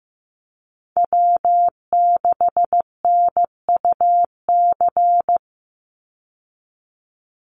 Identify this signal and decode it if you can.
Morse code, heard as a keyed tone, about 14 wpm, 700 Hz. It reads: W6NUC
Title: Call Signs Repeated 3x - 15wpm